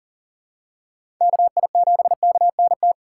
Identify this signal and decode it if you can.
Morse code, heard as a keyed tone, about 40 wpm, 700 Hz. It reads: KI7KNT
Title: USA Call Signs - 6 Characters - 4x Character Spacing - 40wpm